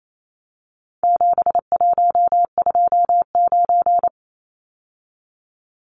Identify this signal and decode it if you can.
Morse code, heard as a keyed tone, about 28 wpm, 700 Hz. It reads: 7129
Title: Numbers - 4-Digit 28wpm